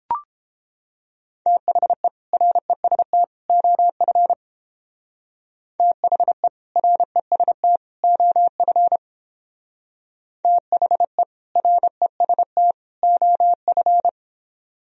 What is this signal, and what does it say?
Morse code, heard as a keyed tone, about 31 wpm, 700 Hz. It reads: THE REST OF THE REST OF THE REST OF